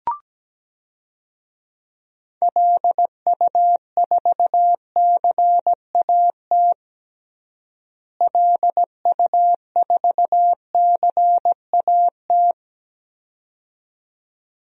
Morse code, heard as a keyed tone, about 17 wpm, 700 Hz. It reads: LU4CAT LU4CAT
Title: DXCC-100 Call Signs - 6 Characters - Repeated 2x - 17wpm